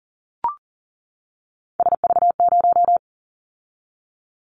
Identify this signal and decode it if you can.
Morse code, heard as a keyed tone, about 38 wpm, 700 Hz. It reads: SV0
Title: Callsign Prefixes for Top 50 DXCC - 40wpm